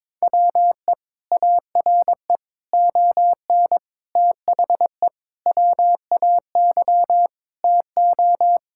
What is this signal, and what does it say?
Morse code, heard as a keyed tone, about 22 wpm, 700 Hz. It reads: WE ARE ON THE WAY TO
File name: practice / 22wpm / Sentences with 4-Word Phrases - 22wpm